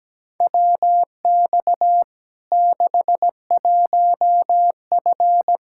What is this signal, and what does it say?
Morse code, heard as a keyed tone, about 17 wpm, 700 Hz. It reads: WX 61F
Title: K1USN SST QSO Practice 17wpm